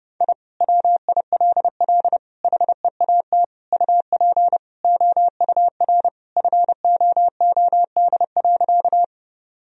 Morse code, heard as a keyed tone, about 31 wpm, 700 Hz. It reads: I WILL HEAT UP OUR FOOD.
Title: Sentences from Top 1000 Words - Review Part 3 - 30wpm